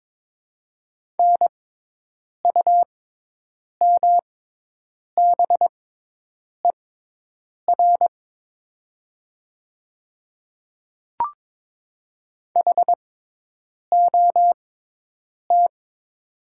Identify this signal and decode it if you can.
Morse code, heard as a keyed tone, about 22 wpm, spaced as for 8 wpm, 700 Hz. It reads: NUMBER HOT
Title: 6x Character Spacing - Top 100 Words - 22wpm